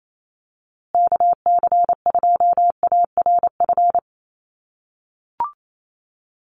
Morse code, heard as a keyed tone, about 28 wpm, 700 Hz. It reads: KC2ARF
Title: USA Call Signs Repeated 2x 28wpm